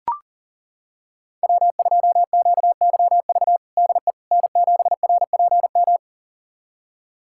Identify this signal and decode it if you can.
Morse code, heard as a keyed tone, about 40 wpm, 700 Hz. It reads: W2QYV DE N7RPM
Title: USA Callsign DE Callsign - 5 Characters - 40wpm